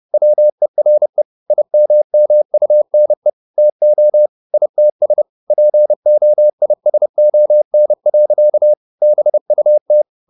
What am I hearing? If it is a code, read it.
Morse code, heard as a keyed tone, about 30 wpm, 600 Hz. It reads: WERE IMMUNE TO ITS POISON. BUT
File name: the old man and the sea / the old man and the sea 1 30wpm